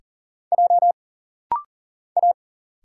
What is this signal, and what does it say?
Morse code, heard as a keyed tone, about 39 wpm, 700 Hz. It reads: J A
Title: Single Letter-Number Alt - Rapid-Fire 40wpm